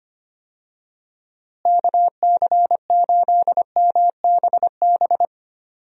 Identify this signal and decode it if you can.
Morse code, heard as a keyed tone, about 25 wpm, 700 Hz. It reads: KC8MBB
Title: USA Call Signs - 6 Characters - Repeated 2x - 25wpm